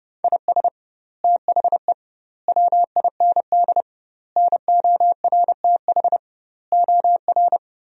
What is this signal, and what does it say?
Morse code, heard as a keyed tone, about 30 wpm, 700 Hz. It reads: IS THE WIND NORTH OR
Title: 2x Word-Spacing - Sentences from Top 400 Words - 30wpm